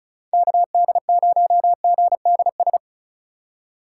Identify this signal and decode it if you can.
Morse code, heard as a keyed tone, about 34 wpm, 700 Hz. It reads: KD0GDS
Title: USA Call Signs - 6 Characters - 4x Character Spacing - 35wpm